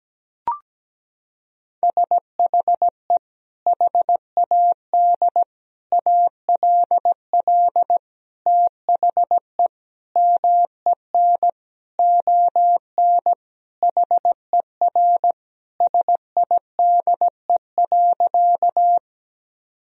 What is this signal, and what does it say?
Morse code, heard as a keyed tone, about 17 wpm, 700 Hz. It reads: SHE HAD ALL THE MEN ON HER SIDE.
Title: Sentences from Top 1000 Words - Review Part 1 - 17wpm